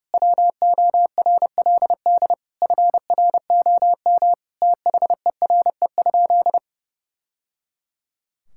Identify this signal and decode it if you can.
Morse code, heard as a keyed tone, about 30 wpm, 700 Hz. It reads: WORLD FROM THERE?
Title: Sentences from Top 1000 Words - Review Part 5 - 30wpm